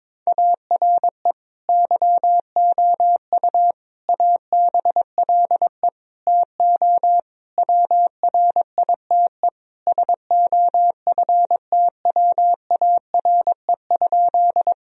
Morse code, heard as a keyed tone, about 22 wpm, 700 Hz. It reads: ARE YOU ABLE TO WRITE SOFTWARE?
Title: Sentences from Top 1000 Words - Review Part 3 - 22wpm